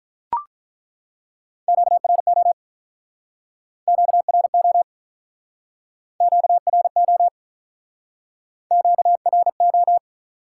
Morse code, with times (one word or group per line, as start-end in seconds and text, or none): 1.7-2.5 QRO
3.9-4.8 QRO
6.2-7.3 QRO
8.7-10.0 QRO